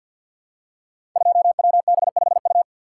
Morse code, heard as a keyed tone, about 53 wpm, 700 Hz. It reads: 2WB5U
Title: Code Groups - 5 Letter:Numbers - 50wpm